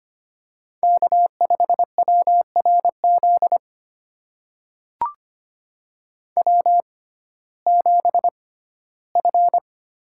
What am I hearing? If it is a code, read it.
Morse code, heard as a keyed tone, about 25 wpm, spaced as for 15 wpm, 700 Hz. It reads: K5WRZ W7F
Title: USA Call Signs - 5 Characters - 6x Character Spacing - 25wpm